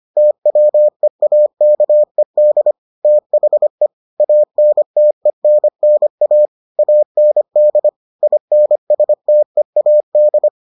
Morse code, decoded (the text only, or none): TWEAKED THE ANTENNA AND INSTEAD